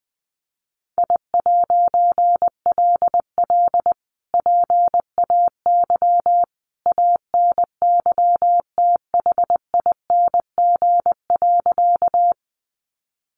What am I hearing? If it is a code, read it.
Morse code, heard as a keyed tone, about 20 wpm, 700 Hz. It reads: I'LL PAY ANYTHING.
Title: A Sound Of Thunder 20wpm